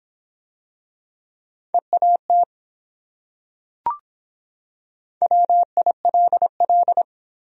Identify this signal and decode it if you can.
Morse code, heard as a keyed tone, about 24 wpm, 700 Hz. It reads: EAT WILL